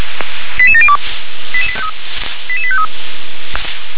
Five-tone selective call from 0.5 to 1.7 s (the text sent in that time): 88702